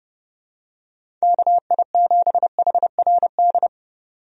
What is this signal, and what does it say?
Morse code, heard as a keyed tone, about 30 wpm, 700 Hz. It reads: KI7HRD
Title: USA Call Signs - 6 Characters - 30wpm